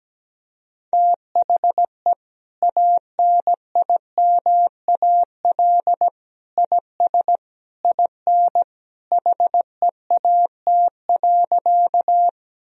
Morse code, heard as a keyed tone, about 17 wpm, 700 Hz. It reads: THE ANIMAL IS IN HEAT.
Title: Sentences from Top 1000 Words - Review Part 3 - 17wpm